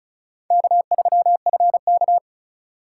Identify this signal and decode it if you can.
Morse code, heard as a keyed tone, about 45 wpm, 700 Hz. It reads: K3FK